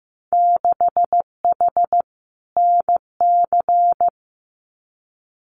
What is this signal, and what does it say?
Morse code, heard as a keyed tone, about 15 wpm, 700 Hz. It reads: 6H NC
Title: Winter Field Day Exchanges Encore 15wpm